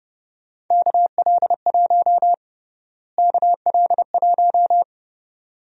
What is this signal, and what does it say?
Morse code, heard as a keyed tone, about 30 wpm, 700 Hz. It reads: KL1 KL1
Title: USA Call Sign Prefixes - Repeated 2x - 30wpm